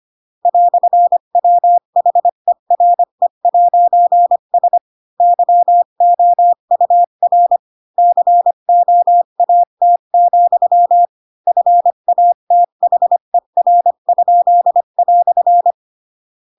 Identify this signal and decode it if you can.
Morse code, heard as a keyed tone, about 25 wpm, 700 Hz. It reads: "WHERE'S YOUR COAT, FATHER?"
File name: Pinocchio / Pinocchio ch8 20 25 ik1hge